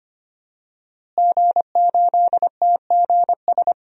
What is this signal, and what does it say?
Morse code, heard as a keyed tone, about 25 wpm, 700 Hz. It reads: G8TGS
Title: DXCC-100 Call Signs - 5 Characters - Repeated 2x - 25wpm